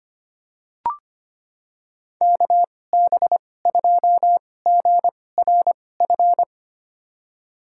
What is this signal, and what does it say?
Morse code, heard as a keyed tone, about 25 wpm, spaced as for 18 wpm, 700 Hz. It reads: KB2GRF
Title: USA Call Signs - 6 Characters - 2x Character Spacing - 25wpm